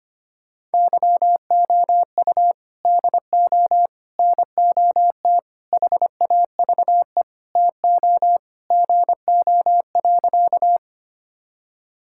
Morse code, heard as a keyed tone, about 25 wpm, 700 Hz. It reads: YOU DO NOT HAVE TO GO.
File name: practice / 25wpm / Sentences with 4-Word Phrases - 25wpm